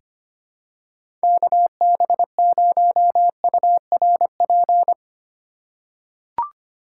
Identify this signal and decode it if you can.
Morse code, heard as a keyed tone, about 25 wpm, 700 Hz. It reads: KB0URP
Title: USA Call Signs - 6 Characters - 8x Character Spacing - 25wpm